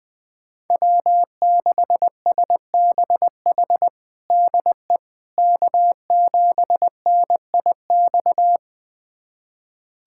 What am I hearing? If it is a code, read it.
Morse code, heard as a keyed tone, about 21 wpm, 700 Hz. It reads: W6SBH DE K7NIX
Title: USA Callsign DE Callsign 20wpm